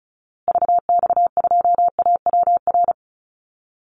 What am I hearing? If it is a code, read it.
Morse code, heard as a keyed tone, about 34 wpm, 700 Hz. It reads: VX2AWR
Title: Call Signs Repeated 2x - 35wpm